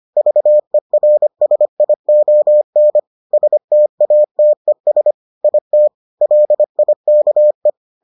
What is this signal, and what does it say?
Morse code, heard as a keyed tone, about 25 wpm, 600 Hz. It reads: VERSION STATES IT LIKE